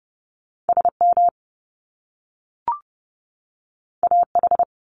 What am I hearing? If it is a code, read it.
Morse code, heard as a keyed tone, about 30 wpm, 700 Hz. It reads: SM AH
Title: Top Two Letter Combinations in Words 30wpm